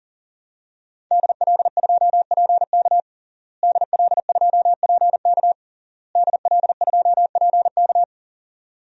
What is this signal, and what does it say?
Morse code, heard as a keyed tone, about 40 wpm, 700 Hz. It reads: DL2PK DL2PK DL2PK
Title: DXCC-100 Call Signs - 5 Characters - Repeated 3x - 40wpm